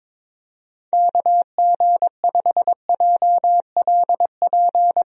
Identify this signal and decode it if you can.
Morse code, heard as a keyed tone, about 23 wpm, 700 Hz. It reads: KG5JLP
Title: USA Call Signs - 6 Characters - 22wpm